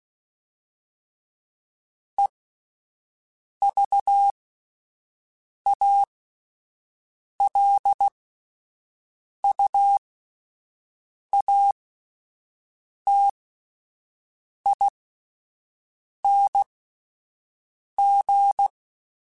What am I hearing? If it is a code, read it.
Morse code, heard as a keyed tone, about 16 wpm, spaced as for 5 wpm, 800 Hz. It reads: EVALUATING